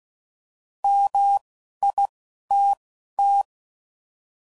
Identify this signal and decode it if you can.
Morse code, heard as a keyed tone, about 16 wpm, spaced as for 11 wpm, 800 Hz. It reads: MITT